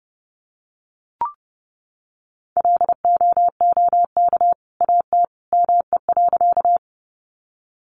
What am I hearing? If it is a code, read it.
Morse code, heard as a keyed tone, about 30 wpm, 700 Hz. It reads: LOOK AT ME.
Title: Sentences from Top 200 Words 30wpm